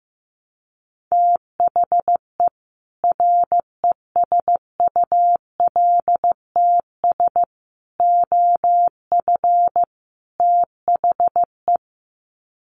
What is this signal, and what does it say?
Morse code, heard as a keyed tone, about 15 wpm, 700 Hz. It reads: THE RESULTS OF THE
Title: Sets of 4 Words 15wpm